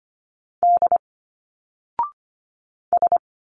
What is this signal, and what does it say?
Morse code, heard as a keyed tone, about 24 wpm, 700 Hz. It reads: D S
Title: Single Letters 25wpm